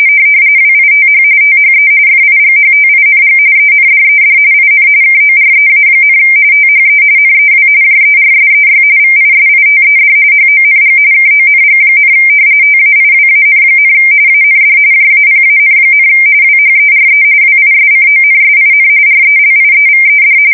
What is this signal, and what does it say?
RTTY, about 45.45 baud. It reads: WITH THE FIGHTING ENDED IN KOREA, THE NEW JERSEY VISITED HONG KONG, TAIWAN, AND JAPAN, WHERE WISCONSIN RELIEVED
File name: fsk170 / battleshipnj